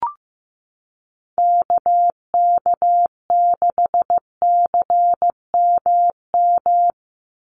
Morse code, heard as a keyed tone, about 15 wpm, 700 Hz. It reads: KK6CMM
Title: USA Call Signs 15wpm